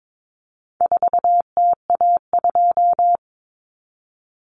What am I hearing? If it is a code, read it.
Morse code, heard as a keyed tone, about 22 wpm, 700 Hz. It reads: 4TA2
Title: Common Cut Numbers - 4-Digit 22wpm